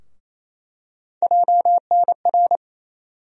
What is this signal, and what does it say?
Morse code, heard as a keyed tone, about 28 wpm, 700 Hz. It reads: JNR
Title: USA Call Sign Postfixes - 3 Letters 28wpm